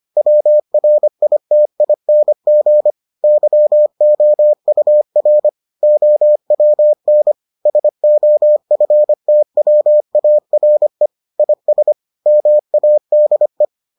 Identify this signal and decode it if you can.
Morse code, heard as a keyed tone, about 25 wpm, 600 Hz. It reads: WRITING YOUR OWN SOFTWARE IS MADE